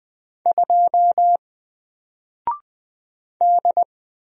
Morse code, heard as a keyed tone, about 20 wpm, 700 Hz. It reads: 2 D
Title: Single Letter-Number Alt 20wpm